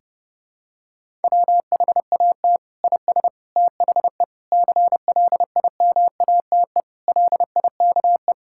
Morse code, heard as a keyed tone, about 31 wpm, 700 Hz. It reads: WHAT IS THE CLIMATE LIKE
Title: Sentences from Top 1600 Words 30wpm